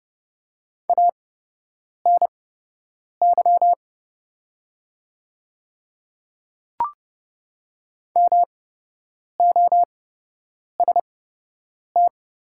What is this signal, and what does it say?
Morse code, heard as a keyed tone, about 29 wpm, spaced as for 8 wpm, 700 Hz. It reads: ANY MOST
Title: 8x Character Spacing - Top 200 Words - 30wpm